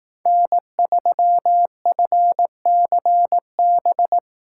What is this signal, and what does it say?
Morse code, heard as a keyed tone, about 18 wpm, 700 Hz. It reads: N3FCB
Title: Speed Racing - USA 1x3 Call Signs 15wpm